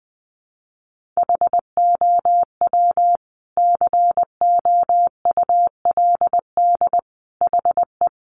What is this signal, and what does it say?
Morse code, heard as a keyed tone, about 20 wpm, 700 Hz. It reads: HOW COULD HE
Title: Sentences from Top 1300 Words 20wpm